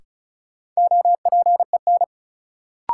Morse code, heard as a keyed tone, about 36 wpm, 700 Hz. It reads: OPEN
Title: Sentences from Top 300 Words - New Word Drill - 35wpm